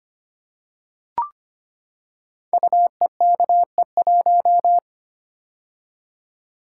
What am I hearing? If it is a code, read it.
Morse code, heard as a keyed tone, about 25 wpm, 700 Hz. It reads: UEKE1
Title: Code Groups - 5 Letter:Numbers - 25wpm